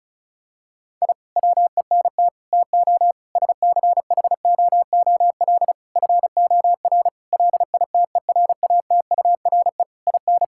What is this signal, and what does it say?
Morse code, heard as a keyed tone, about 34 wpm, 700 Hz. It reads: I WENT TO SCHOOL FOR LITERATURE IN
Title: Sentences from Top 1000 Words - Review Part 6 - 35wpm